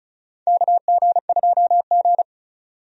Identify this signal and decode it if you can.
Morse code, heard as a keyed tone, about 36 wpm, 700 Hz. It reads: KG2G
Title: USA Call Signs - 4 Characters - 35wpm